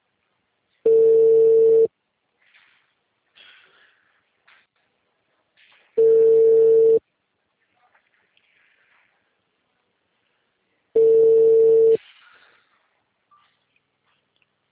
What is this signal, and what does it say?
Ringback tone (China, 450 Hz).
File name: out- -2040- - - .62809